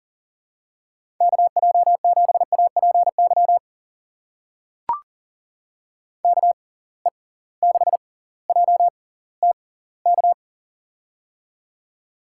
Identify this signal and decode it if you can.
Morse code, heard as a keyed tone, about 40 wpm, spaced as for 20 wpm, 700 Hz. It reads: KJ7APY KE6JTK
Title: USA Call Signs - 6 Characters - 6x Character Spacing - 40wpm